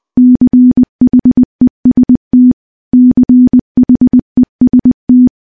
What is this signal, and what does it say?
Morse code, heard as a keyed tone, about 21 wpm, 250 Hz. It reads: CHEST CHEST
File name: morse / morse-training-20WPM